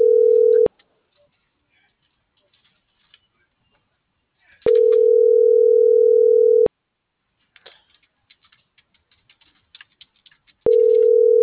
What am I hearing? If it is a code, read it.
Ringback tone (North America), 440 + 480 Hz.
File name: CENTRAL SAGOT / out anyelo1.sagot